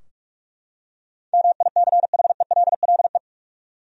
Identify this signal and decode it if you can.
Morse code, heard as a keyed tone, about 46 wpm, 700 Hz. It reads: MICHELLE
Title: USA Common Names - 45wpm